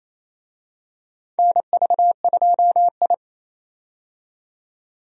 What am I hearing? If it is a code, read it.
Morse code, heard as a keyed tone, about 28 wpm, 700 Hz. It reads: NV2I
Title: USA Call Signs - 4 Characters - 28wpm